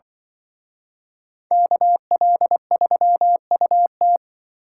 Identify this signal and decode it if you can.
Morse code, heard as a keyed tone, about 23 wpm, 700 Hz. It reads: KL3UT
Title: Speed Racing - USA Call Signs 20wpm